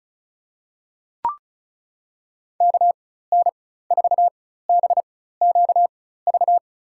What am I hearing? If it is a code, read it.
Morse code, heard as a keyed tone, about 36 wpm, spaced as for 16 wpm, 700 Hz. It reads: KN4BQV
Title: USA Call Signs - 6 Characters - 4x Character Spacing - 35wpm